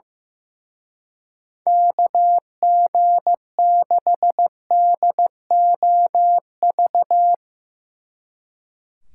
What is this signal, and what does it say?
Morse code, heard as a keyed tone, about 16 wpm, 700 Hz. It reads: KG6DOV
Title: Speed Racing - USA 2x3 Call Signs 15wpm